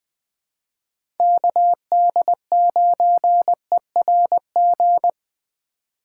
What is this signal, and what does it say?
Morse code, heard as a keyed tone, about 20 wpm, 700 Hz. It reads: KD9ERG
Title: USA Call Signs - 6 Characters - 2x Character Spacing - 20wpm